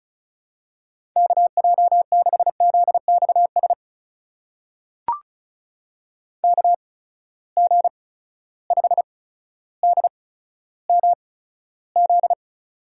Morse code, heard as a keyed tone, about 34 wpm, spaced as for 15 wpm, 700 Hz. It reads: KJ6ZXS KG5DMZ
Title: USA Call Signs - 6 Characters - 8x Character Spacing - 35wpm